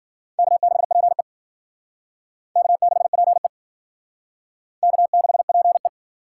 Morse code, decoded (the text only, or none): K6PE K6PE K6PE